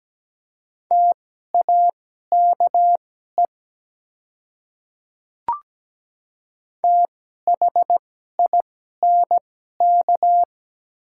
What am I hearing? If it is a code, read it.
Morse code, heard as a keyed tone, about 17 wpm, spaced as for 12 wpm, 700 Hz. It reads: TAKE THINK
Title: 2x Character Spacing - Top 200 Words - 17wpm